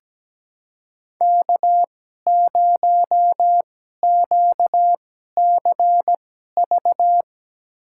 Morse code, heard as a keyed tone, about 17 wpm, spaced as for 12 wpm, 700 Hz. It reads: K0QCV
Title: USA Call Signs - 5 Characters - 2x Character Spacing - 17wpm